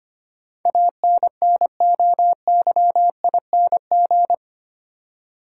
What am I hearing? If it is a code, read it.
Morse code, heard as a keyed tone, about 25 wpm, 690 Hz. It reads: ANNOYING
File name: 8 Letter Words - First 3 Letters Given - 25wpm